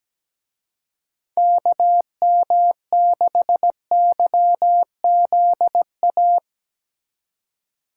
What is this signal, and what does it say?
Morse code, heard as a keyed tone, about 18 wpm, 700 Hz. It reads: KM6YZA
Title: Speed Racing - USA Call Signs 17wpm